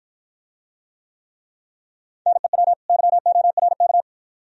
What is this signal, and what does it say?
Morse code, heard as a keyed tone, about 53 wpm, 700 Hz. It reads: NEW YORK